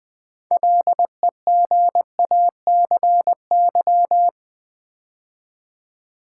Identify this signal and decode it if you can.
Morse code, heard as a keyed tone, about 20 wpm, 700 Hz. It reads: LEGACY
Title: 6 Letter Words - First 3 Letters Given - 20wpm